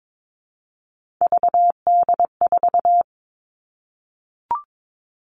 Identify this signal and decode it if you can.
Morse code, heard as a keyed tone, about 22 wpm, 700 Hz. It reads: VD4
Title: All Cut Numbers - 3-Digit 22wpm